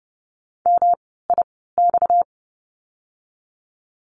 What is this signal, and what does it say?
Morse code, heard as a keyed tone, about 30 wpm, spaced as for 17 wpm, 700 Hz. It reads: MIX
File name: course / 30:10wpm / 230 Lesson 37f - Letter x - Words 30wpm